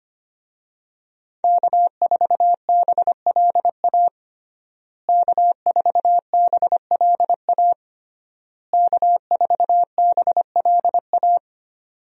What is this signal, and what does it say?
Morse code, heard as a keyed tone, about 25 wpm, 700 Hz. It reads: K4BLA K4BLA K4BLA
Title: USA Call Signs - 5 Characters - Repeated 3x - 25wpm